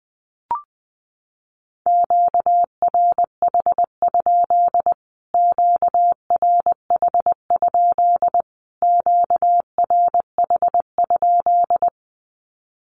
Morse code, heard as a keyed tone, about 19 wpm, 700 Hz. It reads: QRH? QRH? QRH?
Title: Q-Codes Repeated 3x 20wpm